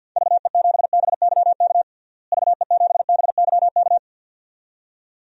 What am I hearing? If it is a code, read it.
Morse code, heard as a keyed tone, about 50 wpm, 700 Hz. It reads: VE7BYK VE7BYK
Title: DXCC-100 Call Signs - 6 Characters - Repeated 2x - 50wpm